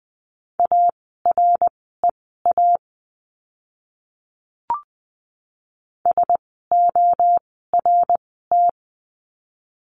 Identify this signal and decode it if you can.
Morse code, heard as a keyed tone, about 20 wpm, spaced as for 14 wpm, 700 Hz. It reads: AREA SORT
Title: 054 Lesson 10f - Letter r - Words 20wpm